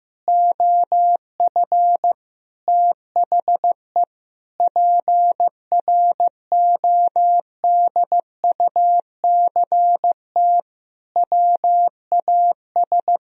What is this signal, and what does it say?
Morse code, heard as a keyed tone, about 15 wpm, 700 Hz. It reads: OF THE PRODUCT WAS
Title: Sentences from Top 1000 Words - Review Part 3 - 15wpm